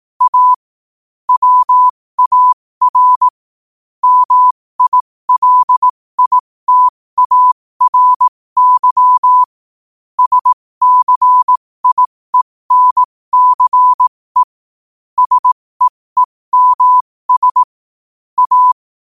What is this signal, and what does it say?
Morse code, heard as a keyed tone, about 18 wpm, spaced as for 15 wpm, 1000 Hz. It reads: A WAR MILITARY SCIENCE SEEMS A
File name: qotdfiles / 15wpm